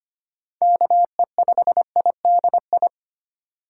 Morse code, heard as a keyed tone, about 25 wpm, 700 Hz. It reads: KE5IDI
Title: USA Call Signs - 6 Characters - 2x Character Spacing - 25wpm